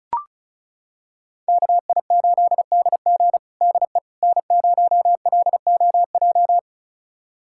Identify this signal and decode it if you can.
Morse code, heard as a keyed tone, about 36 wpm, 700 Hz. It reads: KI8DG DE N0LOJ
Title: USA Callsign DE Callsign - 5 Characters - 35wpm